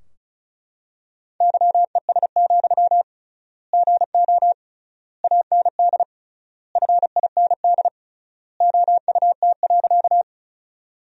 Morse code, heard as a keyed tone, about 34 wpm, 700 Hz. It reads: YES, GO AND FIND OUT.
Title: 3x Word-Spacing - Sentences from Top 500 Words - 35wpm